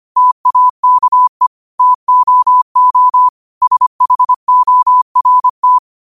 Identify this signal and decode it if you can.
Morse code, heard as a keyed tone, about 25 wpm, 1000 Hz. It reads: TAKE TOO SHORT
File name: qotdfiles / 25wpm